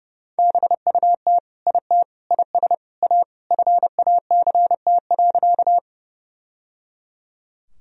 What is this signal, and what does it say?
Morse code, heard as a keyed tone, about 30 wpm, 700 Hz. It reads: BUT IT IS A FACT.
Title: Sentences from Top 1000 Words - Review Part 3 - 30wpm